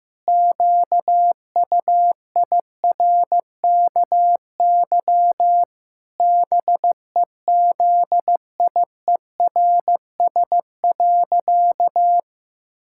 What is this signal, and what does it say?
Morse code, heard as a keyed tone, about 15 wpm, 700 Hz. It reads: QUIRKY BEZIERS.
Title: Pangram Challenge v2 - 15wpm